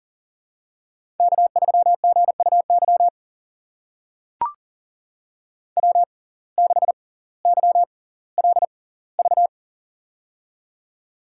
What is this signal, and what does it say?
Morse code, heard as a keyed tone, about 38 wpm, spaced as for 20 wpm, 700 Hz. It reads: K3GUY W6YLV
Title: USA Call Signs - 5 Characters - 6x Character Spacing - 40wpm